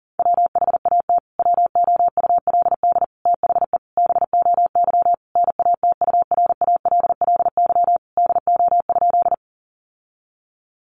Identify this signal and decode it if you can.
Morse code, heard as a keyed tone, about 40 wpm, 700 Hz. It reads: WHAT WOULD THE BOY NATURALLY DO?
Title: Sentences from Top 1300 Words 40wpm